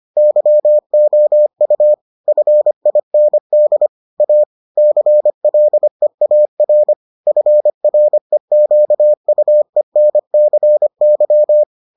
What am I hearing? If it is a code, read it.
Morse code, heard as a keyed tone, about 25 wpm, 600 Hz. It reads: YOU FIND A CLEAR FREQUENCY